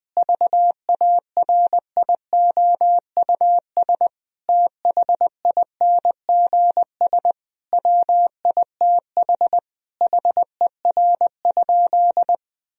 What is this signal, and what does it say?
Morse code, heard as a keyed tone, about 20 wpm, 700 Hz. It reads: VARIOUS THINGS WITH HER?
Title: Sentences from Top 1000 Words - Review Part 4 - 20wpm